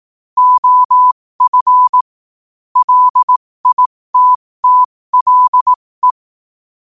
Morse code, heard as a keyed tone, about 18 wpm, spaced as for 15 wpm, 1000 Hz. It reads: OF LITTLE
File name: qotdfiles / 15wpm